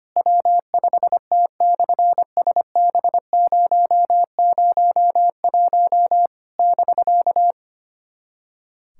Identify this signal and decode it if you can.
Morse code, heard as a keyed tone, about 25 wpm, 700 Hz. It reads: W5T/SB001 <BK>
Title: SOTA QSO Practice 25wpm